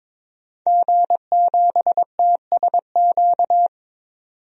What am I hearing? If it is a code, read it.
Morse code, heard as a keyed tone, about 22 wpm, 700 Hz. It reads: G7TSQ
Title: DXCC-100 Call Signs - 5 Characters - 22wpm